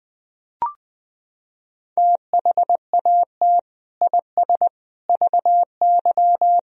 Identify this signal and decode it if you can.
Morse code, heard as a keyed tone, about 21 wpm, 700 Hz. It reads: THAT IS VY
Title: Sentences from Top 400 Words and Abbreviations 20wpm